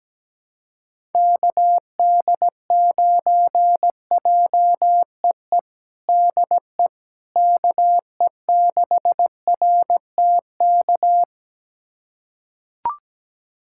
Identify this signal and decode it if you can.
Morse code, heard as a keyed tone, about 17 wpm, 700 Hz. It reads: KD9JEE DE KE6RTK
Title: USA Callsign DE Callsign - 6 Characters - 17wpm